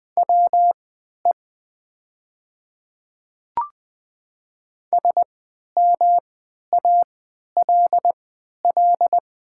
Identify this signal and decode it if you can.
Morse code, heard as a keyed tone, about 20 wpm, spaced as for 11 wpm, 700 Hz. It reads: WE SMALL
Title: 3x Character Spacing - Top 200 Words - 20wpm